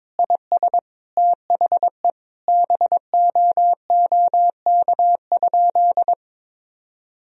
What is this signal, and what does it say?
Morse code, heard as a keyed tone, about 22 wpm, 700 Hz. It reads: IS THE BOOK?
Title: Sentences from Top 1000 Words - Review Part 1 - 22wpm